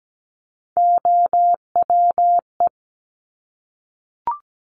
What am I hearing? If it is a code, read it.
Morse code, heard as a keyed tone, about 17 wpm, 700 Hz. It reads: OWE
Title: Top Three Letter Combinations in Words 17wpm